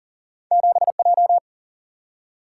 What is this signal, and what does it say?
Morse code, heard as a keyed tone, about 42 wpm, 700 Hz. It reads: 7J
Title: Single-Letter Single-Number 40wpm